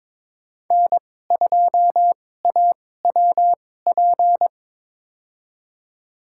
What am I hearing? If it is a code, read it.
Morse code, heard as a keyed tone, about 21 wpm, spaced as for 16 wpm, 700 Hz. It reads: N2AWP
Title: USA Call Signs - 5 Characters - 2x Character Spacing - 22wpm